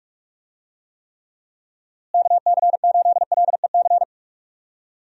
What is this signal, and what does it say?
Morse code, heard as a keyed tone, about 46 wpm, 700 Hz. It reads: KC8LEC
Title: USA Call Signs - 6 Characters - Repeated 3x - 45wpm